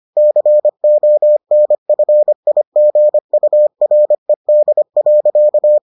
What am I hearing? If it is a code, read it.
Morse code, heard as a keyed tone, about 25 wpm, 600 Hz. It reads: CONFIGURED.